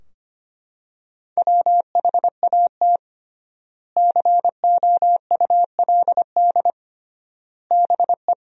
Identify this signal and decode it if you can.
Morse code, heard as a keyed tone, about 25 wpm, 700 Hz. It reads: WHAT COULD BE
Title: 3x Word-Spacing - Sentences from Top 400 Words - 25wpm